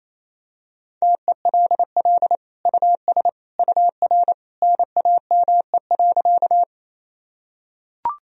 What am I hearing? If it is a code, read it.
Morse code, heard as a keyed tone, about 28 wpm, 700 Hz. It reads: TELL US UR NAME.
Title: Sentences from Top 200 Words and Abbreviations 28wpm